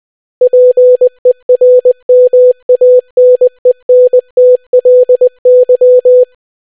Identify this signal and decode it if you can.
Morse code, heard as a keyed tone, about 20 wpm, 500 Hz. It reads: PERMANENTLY